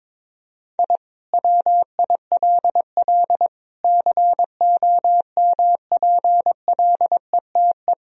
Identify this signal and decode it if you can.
Morse code, heard as a keyed tone, about 21 wpm, 700 Hz. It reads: I WILL COMPLETE
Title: Sentences from Top 1000 Words - Review Part 2 - 22wpm